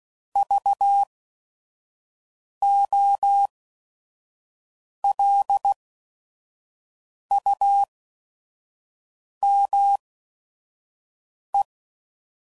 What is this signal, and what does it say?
Morse code, heard as a keyed tone, about 16 wpm, spaced as for 5 wpm, 800 Hz. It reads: VOLUME
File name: Prefixes / By First Letter / Vs